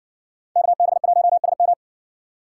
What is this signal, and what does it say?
Morse code, heard as a keyed tone, about 60 wpm, 700 Hz. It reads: KB1SM